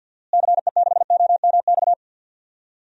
Morse code, heard as a keyed tone, about 52 wpm, 700 Hz. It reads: CE6OMX
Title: DXCC-100 Call Signs - 6 Characters - Repeated 2x - 50wpm